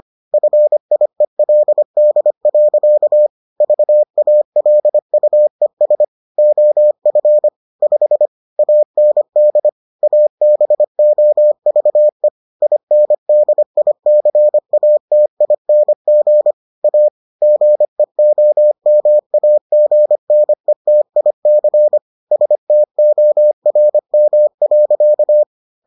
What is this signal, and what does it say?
Morse code, heard as a keyed tone, about 25 wpm, 600 Hz. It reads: FIELD. VALUES OF 5 AND ABOVE INDICATING A GEOMAGNETIC STORM.